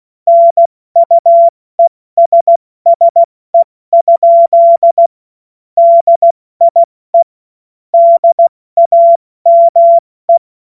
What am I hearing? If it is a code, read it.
Morse code, heard as a keyed tone, about 16 wpm, 680 Hz. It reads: NUESSE? DIE DAME
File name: Morsekurs / Witze / Witz 05. Oktober 2021, 70 F80